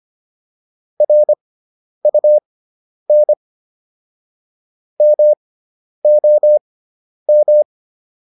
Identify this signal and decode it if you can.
Morse code, heard as a keyed tone, about 25 wpm, spaced as for 10 wpm, 600 Hz. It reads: RUN MOM